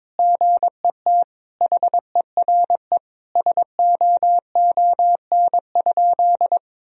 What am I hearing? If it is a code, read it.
Morse code, heard as a keyed tone, about 22 wpm, 700 Hz. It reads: GET HERE SOON?
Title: Sentences from Top 1000 Words - Review Part 3 - 22wpm